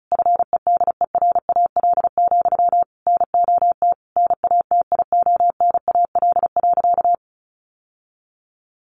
Morse code, heard as a keyed tone, about 34 wpm, 700 Hz. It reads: FEDERAL, NOT NATIONAL.
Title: Sentences from Top 1200 Words 35wpm